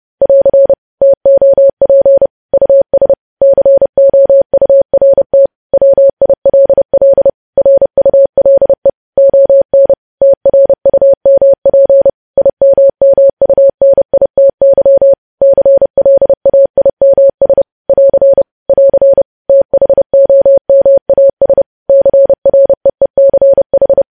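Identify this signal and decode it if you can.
Morse code, heard as a keyed tone, about 30 wpm, 550 Hz. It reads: <AR> TOP US COURT WILL RULE ON TRUMP IMMUNITY CLAIMS <AR> <AR> THOMAS CREECH
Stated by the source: News headlines sent in Morse at 30 wpm, preceded by VVV.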